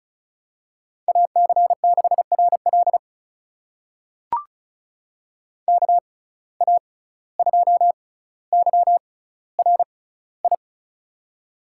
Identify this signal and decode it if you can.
Morse code, heard as a keyed tone, about 36 wpm, spaced as for 17 wpm, 700 Hz. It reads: AC6RL KA2YRI
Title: USA Call Signs - 6x Character Spacing - 35wpm